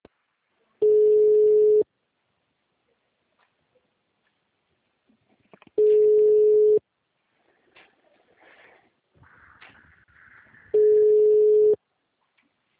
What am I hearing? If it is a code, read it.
Ringback tone (Europe, 425 Hz).